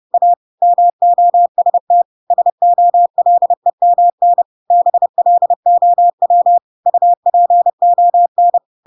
Morse code, heard as a keyed tone, about 30 wpm, 700 Hz. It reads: A MOST SOLEMN BLOW UPON
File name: Pinocchio / Pinocchio ch1 30 30 ik1hge